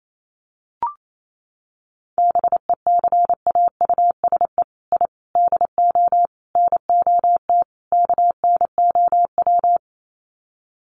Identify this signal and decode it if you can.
Morse code, heard as a keyed tone, about 28 wpm, 700 Hz. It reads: BECAUSE I DO NOT KNOW
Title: Sets of 5 Words Encore2 28wpm